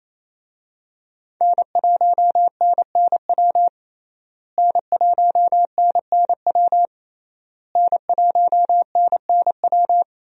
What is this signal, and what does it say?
Morse code, heard as a keyed tone, about 28 wpm, 700 Hz. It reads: N1NNW N1NNW N1NNW
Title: USA Call Signs - 5 Characters - Repeated 3x - 28wpm